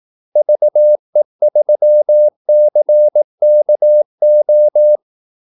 Morse code, heard as a keyed tone, about 18 wpm, 600 Hz. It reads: VE3CKO
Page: Kaldesignaler 18 wpm